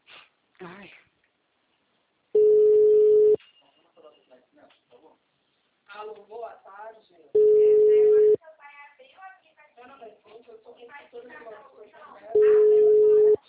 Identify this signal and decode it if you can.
Ringback tone (Europe, 425 Hz), with speech.